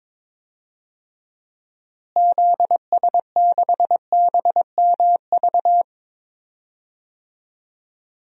Morse code, heard as a keyed tone, about 22 wpm, 700 Hz. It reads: ZS6BMV
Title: DXCC-100 Call Signs - 6 Characters - Repeated 3x - 22wpm